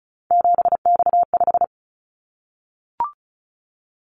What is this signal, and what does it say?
Morse code, heard as a keyed tone, about 36 wpm, 700 Hz. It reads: 7X5
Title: Callsign Prefixes for Top 100 DXCC - 35wpm